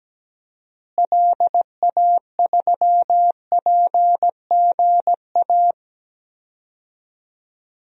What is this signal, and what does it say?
Morse code, heard as a keyed tone, about 17 wpm, 700 Hz. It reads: LA3PGA
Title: DXCC-100 Call Signs - 6 Characters - Repeated 3x - 17wpm